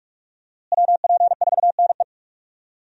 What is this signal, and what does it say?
Morse code, heard as a keyed tone, about 46 wpm, 700 Hz. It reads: WP4NE
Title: USA Call Signs - 5 Characters - Repeated 2x - 45wpm